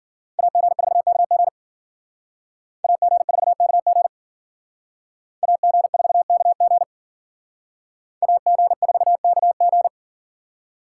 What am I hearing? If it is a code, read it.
Morse code, heard as a keyed tone, about 54 wpm, 700 Hz. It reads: AG4KG AG4KG AG4KG AG4KG